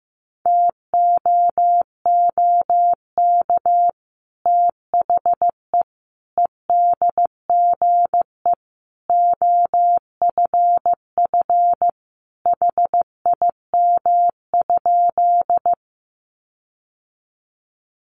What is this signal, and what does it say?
Morse code, heard as a keyed tone, about 15 wpm, 700 Hz. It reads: TOOK THE EDGE OFF HIM?
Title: Sentences from Top 1300 Words 15wpm